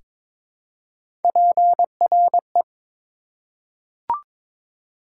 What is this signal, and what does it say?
Morse code, heard as a keyed tone, about 21 wpm, 700 Hz. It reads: PRE
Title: Prefixes and Suffixes 22wpm